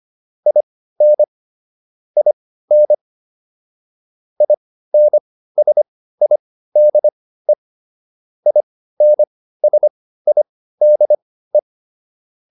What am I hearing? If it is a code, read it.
Morse code, heard as a keyed tone, about 25 wpm, spaced as for 15 wpm, 600 Hz. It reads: IN IN INSIDE INSIDE